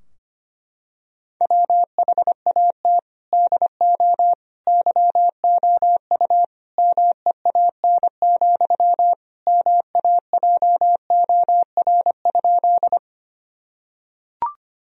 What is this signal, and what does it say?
Morse code, heard as a keyed tone, about 25 wpm, 700 Hz. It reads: WHAT DO YOU MEAN, MAJOR?
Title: Sentences from Top 1000 Words - Review Part 4 - 25wpm